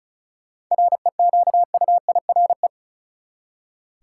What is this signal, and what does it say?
Morse code, heard as a keyed tone, about 36 wpm, 700 Hz. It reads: REQUIRE
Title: Seven Letter Words - 35wpm